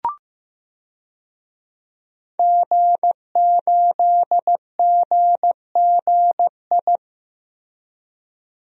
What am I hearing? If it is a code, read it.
Morse code, heard as a keyed tone, about 15 wpm, 700 Hz. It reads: G8GGI
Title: DXCC-100 Call Signs - 5 Characters - Repeated 2x - 15wpm